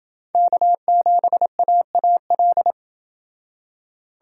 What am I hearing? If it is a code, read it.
Morse code, heard as a keyed tone, about 28 wpm, 700 Hz. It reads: K7AAL